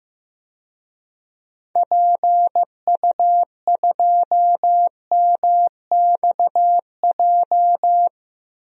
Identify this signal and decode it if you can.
Morse code, heard as a keyed tone, about 15 wpm, 700 Hz. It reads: PU2MXJ
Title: DXCC-100 Call Signs - 6 Characters - Repeated 2x - 15wpm